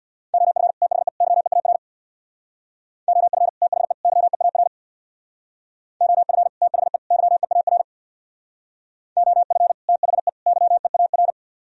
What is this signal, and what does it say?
Morse code, heard as a keyed tone, about 64 wpm, 700 Hz. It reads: OF THE YEAR OF THE YEAR OF THE YEAR OF THE YEAR